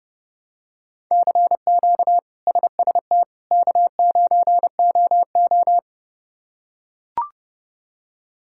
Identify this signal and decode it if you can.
Morse code, heard as a keyed tone, about 29 wpm, 700 Hz. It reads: CQ SST K9OO
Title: K1USN SST QSO Practice - 30wpm